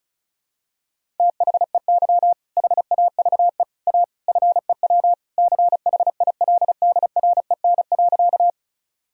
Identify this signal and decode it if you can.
Morse code, heard as a keyed tone, about 36 wpm, 700 Hz. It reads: THEY HAVE A FEW CHILDREN.
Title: Sentences from Top 1000 Words - Review Part 1 - 35wpm